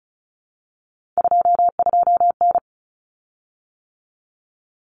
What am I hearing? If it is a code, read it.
Morse code, heard as a keyed tone, about 34 wpm, 700 Hz. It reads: 22N
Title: Common Cut Numbers - 3-Digit 35wpm